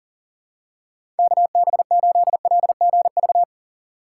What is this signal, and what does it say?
Morse code, heard as a keyed tone, about 40 wpm, 700 Hz. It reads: KB8LGV
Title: USA Call Signs - 6x Character Spacing - 40wpm